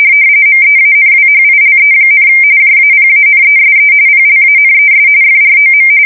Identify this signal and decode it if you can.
RTTY, about 45.45 baud. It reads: MEANWHILE, JAPANESE PLANES FROM
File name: fsk170 / battleshipnj